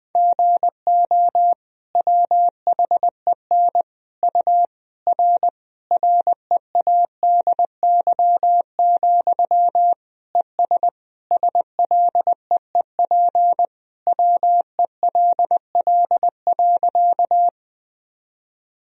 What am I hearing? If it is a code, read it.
Morse code, heard as a keyed tone, about 20 wpm, 700 Hz. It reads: GO WHEN U R READY, ES SLEEP WELL.
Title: Sentences from Top 500 Words and Abbreviations 20wpm